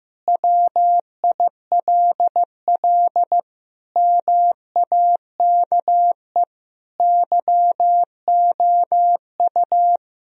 Morse code, heard as a keyed tone, about 16 wpm, 700 Hz. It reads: WILL MAKE YOU
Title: Sentences from Top 1000 Words - Review Part 3 - 15wpm